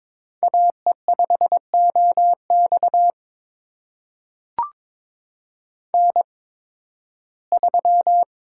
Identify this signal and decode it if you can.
Morse code, heard as a keyed tone, about 22 wpm, spaced as for 14 wpm, 700 Hz. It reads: AE5OX N3
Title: USA Call Signs - 8x Character Spacing - 22wpm